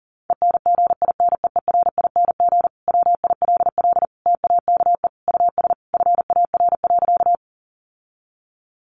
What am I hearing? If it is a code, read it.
Morse code, heard as a keyed tone, about 40 wpm, 700 Hz. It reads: ENGINEERING WILL TAKE US FAR.
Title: Sentences from Top 1500 Words 40wpm